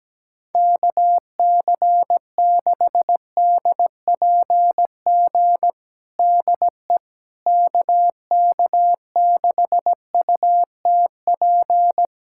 Morse code, heard as a keyed tone, about 17 wpm, 700 Hz. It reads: KC6DPG DE KK6UTP
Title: USA Callsign DE Callsign - 6 Characters - 17wpm